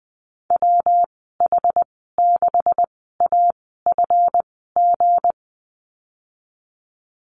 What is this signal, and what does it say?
Morse code, heard as a keyed tone, about 20 wpm, spaced as for 15 wpm, 700 Hz. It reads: WH6AFG
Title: 173 Lesson 28g - Number 9 - Call Signs 20wpm